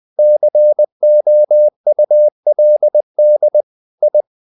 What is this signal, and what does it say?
Morse code, heard as a keyed tone, about 20 wpm, 600 Hz. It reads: COULD I
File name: the old man and the sea part 2 20wpm